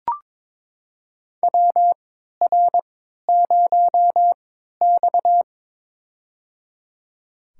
Morse code, heard as a keyed tone, about 22 wpm, spaced as for 13 wpm, 700 Hz. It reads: WR0X
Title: USA Call Signs - 4 Characters - 3x Character Spacing - 22wpm